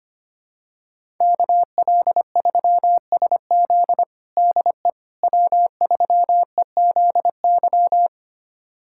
Morse code, heard as a keyed tone, about 25 wpm, 700 Hz. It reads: KL3SZ DE W3EZY
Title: USA Callsign DE Callsign - 5 Characters - 25wpm